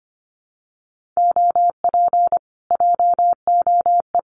Morse code, heard as a keyed tone, about 25 wpm, 700 Hz. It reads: OP JOE
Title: Common QSO Phrases Encore 25wpm